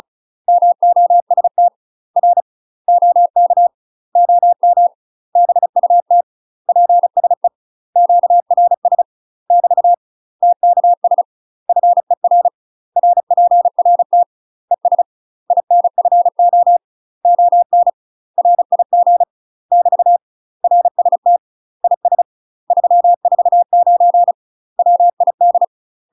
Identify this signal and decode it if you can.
Morse code, heard as a keyed tone, about 36 wpm, 700 Hz. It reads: MOST R OK OM BUT PSE QRS <BT> TKS FER RPRT ES INFO ON RIG <BT> RST IS 349 WID
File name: UK MASTER QSOs TRAINING / 35WPM / UK MASTER QSOs 0038